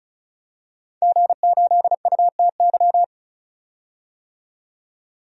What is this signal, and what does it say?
Morse code, heard as a keyed tone, about 36 wpm, 700 Hz. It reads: G8UTY
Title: DXCC-100 Call Signs - 5 Characters - 35wpm